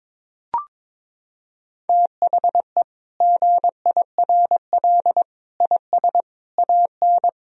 Morse code, heard as a keyed tone, about 22 wpm, 700 Hz. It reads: THE GIRL IS AN
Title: Sentences from Top 1000 Words - Review Part 4 - 22wpm